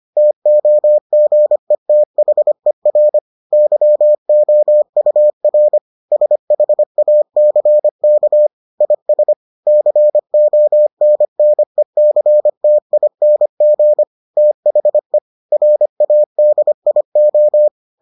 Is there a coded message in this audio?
Yes: Morse code from 0.2 to 17.7 s, heard as a keyed tone, about 25 wpm, 600 Hz. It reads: TOGETHER YOUR SHACK IS CONNECTING THE RADIO